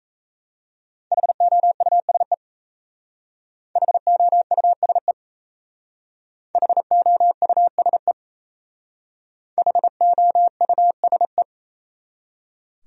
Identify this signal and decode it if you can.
Morse code, heard as a keyed tone, about 36 wpm, 700 Hz. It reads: HOUSE HOUSE HOUSE HOUSE